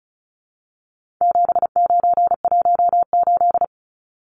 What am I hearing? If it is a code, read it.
Morse code, heard as a keyed tone, about 34 wpm, 700 Hz. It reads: 7918
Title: Numbers - 4-Digit 35wpm